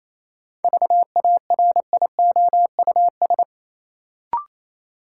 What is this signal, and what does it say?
Morse code, heard as a keyed tone, about 28 wpm, 700 Hz. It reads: VARIOUS
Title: Sentences from Top 1000 Words - Review Part 4 - New Word Drill - 28wpm